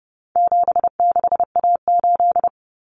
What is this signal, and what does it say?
Morse code, heard as a keyed tone, about 30 wpm, 700 Hz. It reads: 76A8
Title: Common Cut Numbers - 4-Digit 30wpm